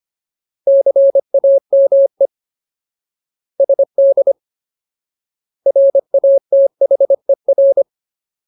Morse code, heard as a keyed tone, about 25 wpm, 550 Hz. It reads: CAME SD RATHER
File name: Ultimate Word List / 25WPM Ultimate